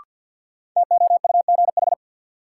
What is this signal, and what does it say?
Morse code, heard as a keyed tone, about 53 wpm, 700 Hz. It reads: TOUGH
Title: Sentences from Top 1200 Words - New Word Drill - 50wpm